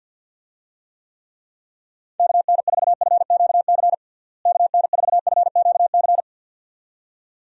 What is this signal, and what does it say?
Morse code, heard as a keyed tone, about 50 wpm, 700 Hz. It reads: KN4FQC KN4FQC
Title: USA Call Signs - 6 Characters - Repeated 2x - 50wpm